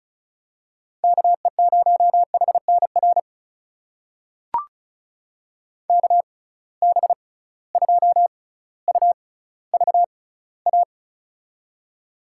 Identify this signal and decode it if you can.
Morse code, heard as a keyed tone, about 34 wpm, spaced as for 18 wpm, 700 Hz. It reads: KE0HNR KB2UVA
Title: USA Call Signs - 6 Characters - 6x Character Spacing - 35wpm